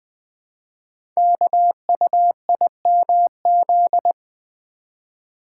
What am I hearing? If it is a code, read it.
Morse code, heard as a keyed tone, about 20 wpm, 700 Hz. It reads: KUIMZ
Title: Code Groups - 5 Letters - 20wpm